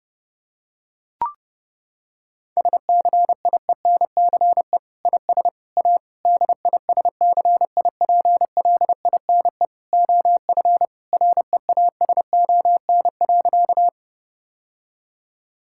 Morse code, heard as a keyed tone, about 30 wpm, 700 Hz. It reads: SCIENCE IS A DISCIPLINE OF REASON.
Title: Sentences from Top 1000 Words - Review Part 6 - 30wpm